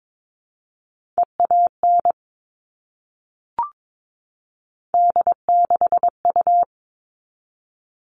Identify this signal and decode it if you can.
Morse code, heard as a keyed tone, about 22 wpm, 700 Hz. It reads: EAN D6U
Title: All Cut Numbers - 3-Digit 22wpm